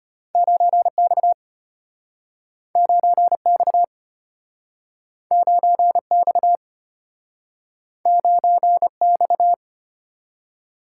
Morse code, heard as a keyed tone, about 34 wpm, 700 Hz. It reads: 9X 9X 9X 9X